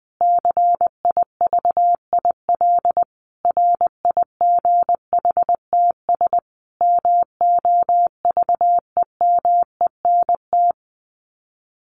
Morse code, heard as a keyed tone, about 20 wpm, 700 Hz. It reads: CIVIL RIGHTS MOVEMENT
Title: Sets of 5 Words Encore2 20wpm